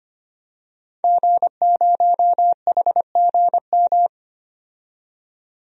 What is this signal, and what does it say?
Morse code, heard as a keyed tone, about 25 wpm, 700 Hz. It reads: G0HGM
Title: DXCC-100 Call Signs - 5 Characters - 25wpm